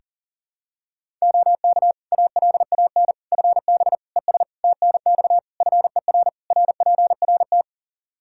Morse code, heard as a keyed tone, about 37 wpm, 700 Hz. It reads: OK ALAN FB ES TNX FER RPRT
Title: Practice QSO - Exchange 2 - Info - 40wpm